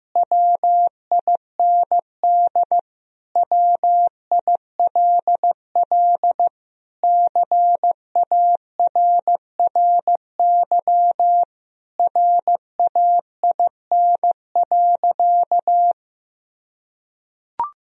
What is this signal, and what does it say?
Morse code, heard as a keyed tone, about 16 wpm, 700 Hz. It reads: WIND WILL CARRY RAIN.
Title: Sentences from Top 1000 Words - Review Part 3 - 15wpm